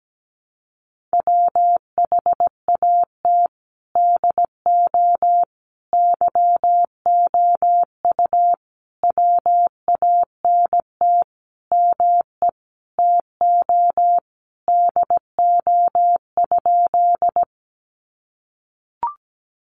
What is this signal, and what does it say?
Morse code, heard as a keyed tone, about 17 wpm, 700 Hz. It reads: WHAT DO YOU WANT ME TO DO?
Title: Sentences from Top 200 Words 17wpm